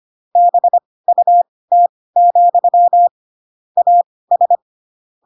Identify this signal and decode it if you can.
Morse code, heard as a keyed tone, about 25 wpm, spaced as for 18 wpm, 700 Hz. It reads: BUT, AS